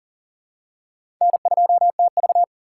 Morse code, heard as a keyed tone, about 40 wpm, 700 Hz. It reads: N2TV
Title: USA Call Signs - 4 Characters - 3x Character Spacing - 40wpm